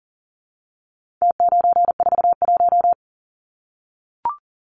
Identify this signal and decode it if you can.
Morse code, heard as a keyed tone, about 40 wpm, 700 Hz. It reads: T941
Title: Numbers - Contest Style - 4-Digit 40wpm